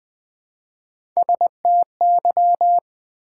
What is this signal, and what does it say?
Morse code, heard as a keyed tone, about 20 wpm, 700 Hz. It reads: STY
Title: Build A Word - 1 Letter At A Time - 5-Letter Words - 20wpm